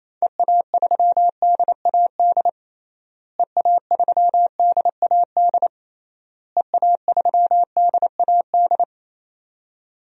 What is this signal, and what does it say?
Morse code, heard as a keyed tone, about 28 wpm, 700 Hz. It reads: EA3DAD EA3DAD EA3DAD
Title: DXCC-100 Call Signs - 6 Characters - Repeated 3x - 28wpm